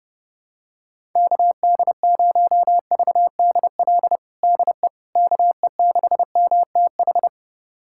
Morse code, heard as a keyed tone, about 30 wpm, 700 Hz. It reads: KD0VDL DE KE6MTH
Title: USA Callsign DE Callsign - 6 Characters - 30wpm